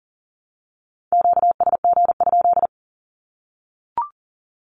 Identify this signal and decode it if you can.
Morse code, heard as a keyed tone, about 40 wpm, 700 Hz. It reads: QSG?
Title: Q-Codes 40wpm